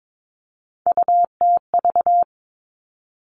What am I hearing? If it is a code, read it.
Morse code, heard as a keyed tone, about 22 wpm, 700 Hz. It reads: UTV
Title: All Cut Numbers - 3-Digit 22wpm